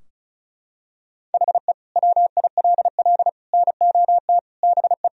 Morse code, heard as a keyed tone, about 36 wpm, 700 Hz. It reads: HE WILL NOT BE
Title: Sentences from Top 1000 Words - Review Part 4 - 35wpm